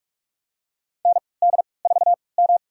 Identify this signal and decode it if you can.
Morse code, heard as a keyed tone, about 46 wpm, spaced as for 25 wpm, 700 Hz. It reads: ND4M
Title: USA Call Signs - 4 Characters - 3x Character Spacing - 45wpm